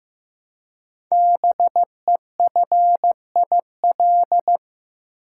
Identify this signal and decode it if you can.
Morse code, heard as a keyed tone, about 15 wpm, 700 Hz. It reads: BEFIL
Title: Code Groups - 5 Letters - 15wpm